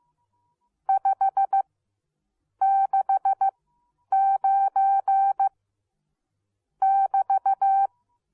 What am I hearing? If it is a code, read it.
Morse code, heard as a keyed tone, about 15 wpm, spaced as for 8 wpm, 780 Hz. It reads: 569 <BT>